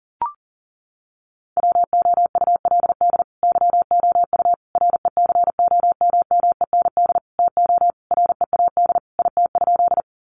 Morse code, heard as a keyed tone, about 40 wpm, 700 Hz. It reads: WOULD YOU RECOMMEND TO READ IT?
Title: Sentences from Top 1200 Words 40wpm